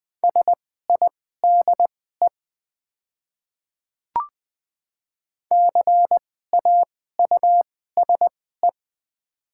Morse code, heard as a keyed tone, about 20 wpm, spaced as for 14 wpm, 700 Hz. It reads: SIDE CAUSE
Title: 2x Character Spacing - Top 200 Words - 20wpm